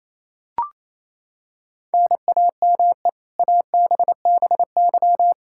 Morse code, heard as a keyed tone, about 28 wpm, 700 Hz. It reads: NAME ABBY
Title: Practice QSO - Exchange 1 - Beginning - 28wpm